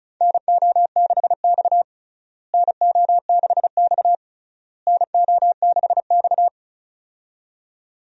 Morse code, heard as a keyed tone, about 34 wpm, 700 Hz. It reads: NO6X NO6X NO6X
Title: USA Call Signs - 4 Characters - Repeated 3x - 35wpm